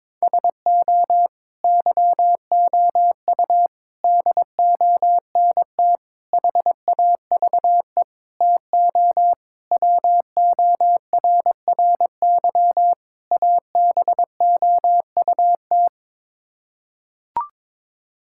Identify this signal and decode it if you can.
Morse code, heard as a keyed tone, about 22 wpm, 700 Hz. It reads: SO YOU DONT HAVE TO WORRY ABOUT
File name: practice / 22wpm / Sets of 7 Words - 22wpm